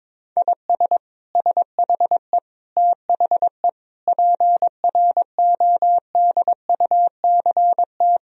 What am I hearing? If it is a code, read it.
Morse code, heard as a keyed tone, about 22 wpm, 700 Hz. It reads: IS SHE THE PRODUCT
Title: Sentences from Top 1000 Words - Review Part 2 - 22wpm